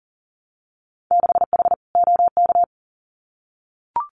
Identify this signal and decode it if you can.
Morse code, heard as a keyed tone, about 40 wpm, 700 Hz. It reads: 6H OK
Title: Winter Field Day Exchanges Encore 40wpm